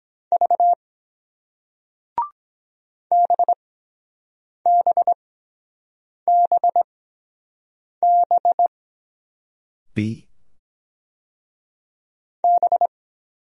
Morse code, heard as a keyed tone, about 23 wpm, spaced as for 7 wpm, 700 Hz. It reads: V BBBB B